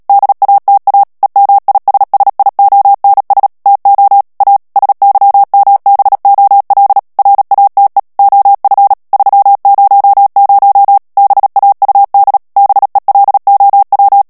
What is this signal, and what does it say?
Morse code, heard as a keyed tone, about 37 wpm, 800 Hz. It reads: DATA EMISSIONS TO A SYMBOL RATE OF 300 BAUD BELOW
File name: morse mp3 / ANGLAIS texte clair 3 vitesse 35 mots minute